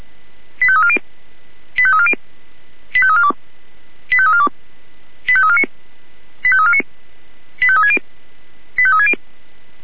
Five-tone selective call (ZVEI), heard as quick runs of five tones, tones of about 70 ms. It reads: 85340 85340 85342 85342 85349 85349 85360 85360
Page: SDR/ZVEI